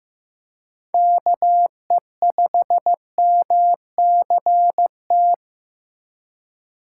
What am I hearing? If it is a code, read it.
Morse code, heard as a keyed tone, about 15 wpm, 700 Hz. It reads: KE5MCT
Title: USA Call Signs - 6 Characters - 4x Character Spacing - 15wpm